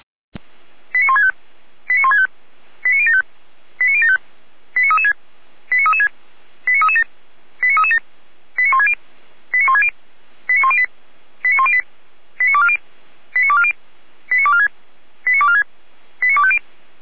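Five-tone selective call (ZVEI), heard as quick runs of five tones, tones of about 70 ms. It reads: 89165 89165 89075 89075 89336 89336 89337 89337 89160 89160 89108 89108 89240 89240 89246 89246 89250
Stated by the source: ZVEI five-tone selective call 89073 received via SDR, from the Feuerwehr Orlamünde fire brigade audio server.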